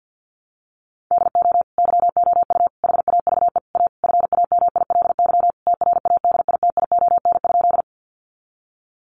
Morse code, heard as a keyed tone, about 50 wpm, 700 Hz. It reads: DO YOU HAVE A FAMILY TRADITION?
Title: Sentences from Top 1500 Words 50wpm